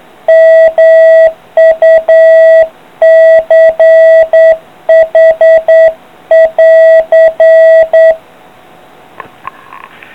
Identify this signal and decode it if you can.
Morse code, heard as a keyed tone, about 9 wpm, 650 Hz. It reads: MUCH <AR>